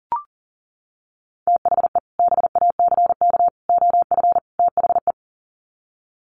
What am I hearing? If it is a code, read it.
Morse code, heard as a keyed tone, about 40 wpm, 700 Hz. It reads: THE BACK OF THE
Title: Sets of 4 Words 40wpm